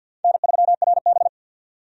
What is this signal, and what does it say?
Morse code, heard as a keyed tone, about 52 wpm, 700 Hz. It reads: N3RB
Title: USA Call Signs - 4 Characters - 50wpm